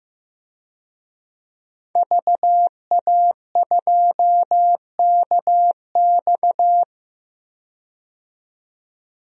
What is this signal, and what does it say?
Morse code, heard as a keyed tone, about 15 wpm, 700 Hz. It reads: VA2KX
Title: DXCC-100 Call Signs - 5 Characters - 15wpm